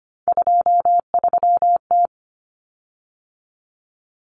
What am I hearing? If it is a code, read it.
Morse code, heard as a keyed tone, about 25 wpm, 700 Hz. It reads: 23T
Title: Common Cut Numbers - 3-Digit 25wpm